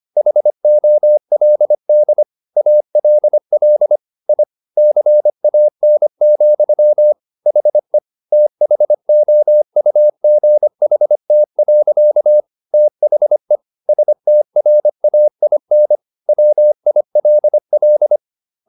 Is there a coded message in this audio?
Yes: Morse code from 0.2 to 18.2 s, heard as a keyed tone, about 25 wpm, 600 Hz. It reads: HOLD ALL I CAN, HE THOUGHT. THE STRAIN WILL